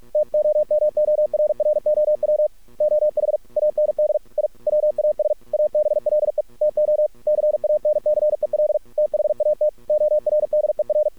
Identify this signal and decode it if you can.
Morse code, heard as a keyed tone, about 46 wpm, 600 Hz. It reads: TOMORROW OH AND I WAS ABLE TO CANCEL THAT ORDER
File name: archived / CRSnet